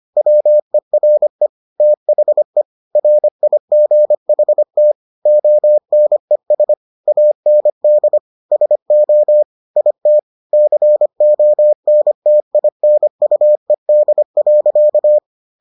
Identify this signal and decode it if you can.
Morse code, heard as a keyed tone, about 25 wpm, 600 Hz. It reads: WERE THE RIGHT ONES AND SO IT CONTINUED.